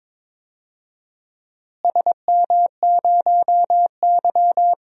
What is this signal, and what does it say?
Morse code, heard as a keyed tone, about 21 wpm, 700 Hz. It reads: SM0Y
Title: DXCC-100 Call Signs - 4 Characters - Repeated 2x - 22wpm